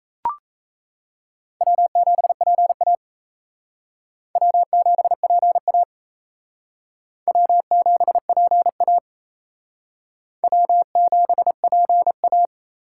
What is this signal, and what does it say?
Morse code, heard as a keyed tone, about 36 wpm, 700 Hz. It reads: W7PA W7PA W7PA W7PA